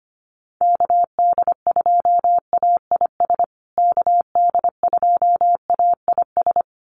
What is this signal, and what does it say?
Morse code, heard as a keyed tone, about 25 wpm, 700 Hz. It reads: KD2AIS KD2AIS
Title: USA Call Signs Repeated 2x 25wpm